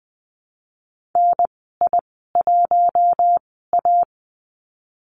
Morse code, heard as a keyed tone, about 20 wpm, spaced as for 14 wpm, 700 Hz. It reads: NI1A
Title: 048 Lesson 9g - Number 4 - Call Signs 20wpm